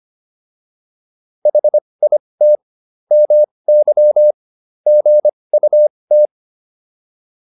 Morse code, heard as a keyed tone, about 25 wpm, spaced as for 20 wpm, 600 Hz. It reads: HIT MY GUT